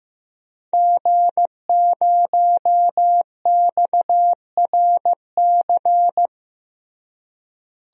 Morse code, heard as a keyed tone, about 15 wpm, 700 Hz. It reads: G0XRC
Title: DXCC-100 Call Signs - 5 Characters - Repeated 2x - 15wpm